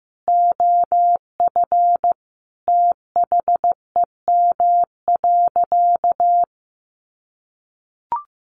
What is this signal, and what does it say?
Morse code, heard as a keyed tone, about 15 wpm, 700 Hz. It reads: OF THEM.
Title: Sentences from Top 1100 Words 15wpm